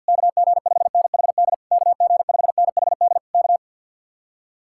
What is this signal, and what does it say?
Morse code, heard as a keyed tone, about 64 wpm, 700 Hz. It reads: KG5NHD KG5NHD K